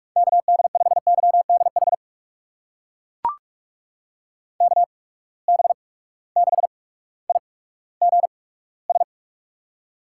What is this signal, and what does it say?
Morse code, heard as a keyed tone, about 42 wpm, spaced as for 19 wpm, 700 Hz. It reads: KD5YDH KB6IGS
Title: USA Call Signs - 6 Characters - 8x Character Spacing - 45wpm